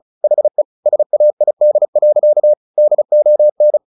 Morse code, heard as a keyed tone, about 34 wpm, 600 Hz. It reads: HE SAID. DON
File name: the old man and the sea 2 35wpm